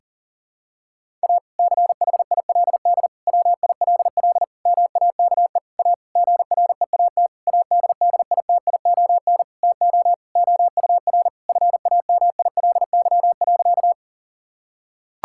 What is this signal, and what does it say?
Morse code, heard as a keyed tone, about 40 wpm, 700 Hz. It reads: A CHILD WILL MAKE A GREAT ADDITION TO OUR FAMILY.
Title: Sentences from Top 1000 Words - Review Part 6 - 40wpm